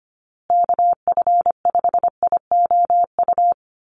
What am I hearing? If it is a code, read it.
Morse code, heard as a keyed tone, about 25 wpm, 700 Hz. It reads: KF5IOU
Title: USA Call Signs Repeated 2x 25wpm